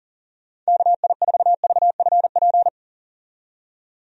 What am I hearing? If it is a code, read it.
Morse code, heard as a keyed tone, about 40 wpm, 700 Hz. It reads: KI4VFP
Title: USA Call Signs - 6 Characters - 40wpm